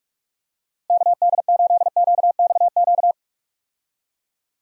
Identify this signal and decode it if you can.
Morse code, heard as a keyed tone, about 43 wpm, 700 Hz. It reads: KD8QXQ
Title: USA Call Signs - 6 Characters - 45wpm